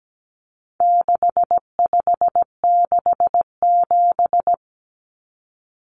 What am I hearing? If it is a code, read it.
Morse code, heard as a keyed tone, about 17 wpm, 700 Hz. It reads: 6567
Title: Numbers - 4-Digit 17wpm